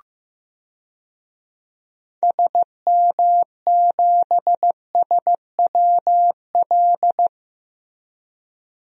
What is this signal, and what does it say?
Morse code, heard as a keyed tone, about 16 wpm, 700 Hz. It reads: SM7SWL
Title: DXCC-100 Call Signs - 6 Characters - Repeated 2x - 15wpm